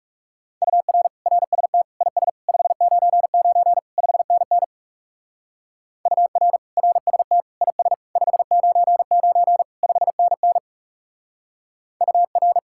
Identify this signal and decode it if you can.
Morse code, heard as a keyed tone, about 36 wpm, 700 Hz. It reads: UR RST IS 599 5NN UR RST IS 599 5NN UR
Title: Speed-Racing Common QSO Phrases Encore - 30wpm